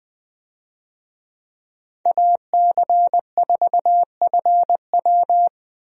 Morse code, heard as a keyed tone, about 20 wpm, 700 Hz. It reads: AC4FW
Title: USA Call Signs - 5 Characters - Repeated 3x - 20wpm